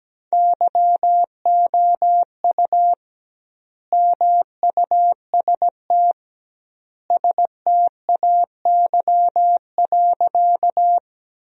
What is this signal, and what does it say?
Morse code, heard as a keyed tone, about 17 wpm, 700 Hz. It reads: YOU MUST STAY.
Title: 2x Word-Spacing - Sentences from Top 500 Words - 17wpm